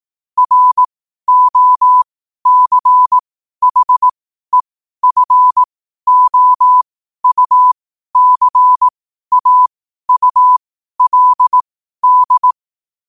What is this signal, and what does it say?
Morse code, heard as a keyed tone, about 18 wpm, spaced as for 12 wpm, 1000 Hz. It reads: ROCHEFOUCAULD
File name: qotdfiles / 12wpm